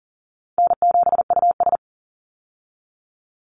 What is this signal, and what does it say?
Morse code, heard as a keyed tone, about 40 wpm, 700 Hz. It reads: N7US
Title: Call Signs Repeated 2x - 40wpm